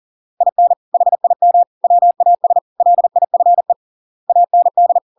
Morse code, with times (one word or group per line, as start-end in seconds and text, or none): 0.4-0.7 IN
0.9-1.6 HIM
1.8-2.6 WAS
2.8-3.7 LIFE
4.3-5.0 AND